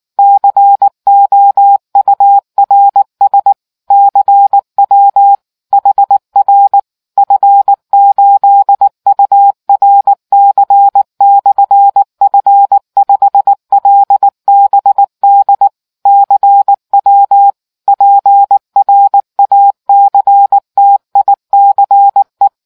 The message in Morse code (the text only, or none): COURS CW HR F8URC/F5LBD CW PRACTICE